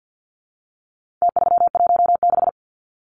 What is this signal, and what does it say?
Morse code, heard as a keyed tone, about 50 wpm, 700 Hz. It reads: T316
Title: Numbers - Contest Style - 4-Digit 50wpm